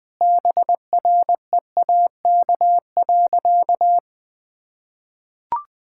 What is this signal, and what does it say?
Morse code, heard as a keyed tone, about 20 wpm, 700 Hz. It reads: BREAK.
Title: Sentences with 4-Word Phrases - 20wpm